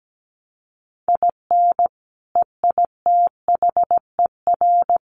Morse code, heard as a keyed tone, about 17 wpm, 700 Hz. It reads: IN EITHER
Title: Sentences from Top 1300 Words 17wpm